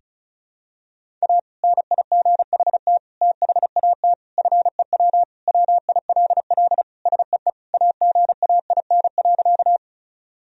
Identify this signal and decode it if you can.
Morse code, heard as a keyed tone, about 34 wpm, 700 Hz. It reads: A NIGHT THAT FEW WILL SEE AGAIN.
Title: Sentences from Top 1000 Words - Review Part 1 - 35wpm